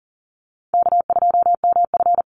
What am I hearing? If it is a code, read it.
Morse code, heard as a keyed tone, about 40 wpm, 700 Hz. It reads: K2MF
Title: Easy Call Signs Repeated 3x - 40wpm